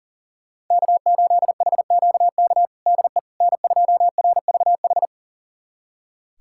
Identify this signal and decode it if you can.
Morse code, heard as a keyed tone, about 37 wpm, 700 Hz. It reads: K8HQK DE N2RVH
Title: USA Callsign DE Callsign - 5 Characters - 40wpm